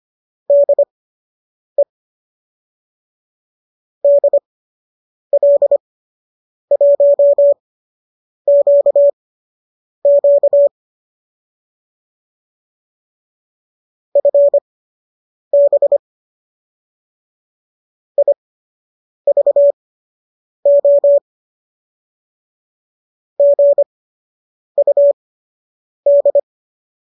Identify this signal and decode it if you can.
Morse code, heard as a keyed tone, about 25 wpm, spaced as for 8 wpm, 570 Hz. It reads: DE DL1QQ FB IVO GUD